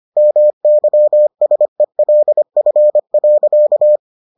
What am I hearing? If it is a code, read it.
Morse code, heard as a keyed tone, about 25 wpm, 600 Hz. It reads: MYSELF.